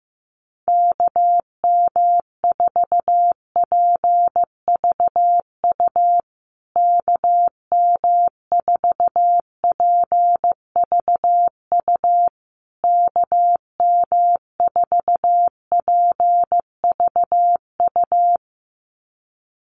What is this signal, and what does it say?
Morse code, heard as a keyed tone, about 15 wpm, 700 Hz. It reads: KM4PVU KM4PVU KM4PVU
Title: USA Call Signs Repeated 3x 15wpm